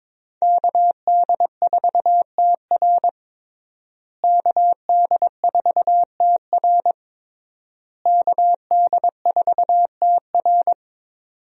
Morse code, heard as a keyed tone, about 22 wpm, 700 Hz. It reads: KD4TR KD4TR KD4TR
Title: USA Call Signs - 5 Characters - Repeated 3x - 22wpm